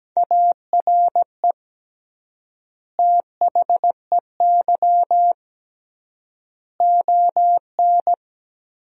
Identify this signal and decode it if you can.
Morse code, heard as a keyed tone, about 17 wpm, 700 Hz. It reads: ARE THEY ON
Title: 3x Word-Spacing - Sentences from Top 500 Words - 17wpm